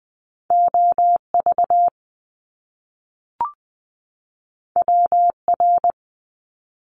Morse code, heard as a keyed tone, about 20 wpm, 700 Hz. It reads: OV WR
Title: Top Two Letter Combinations in Words 20wpm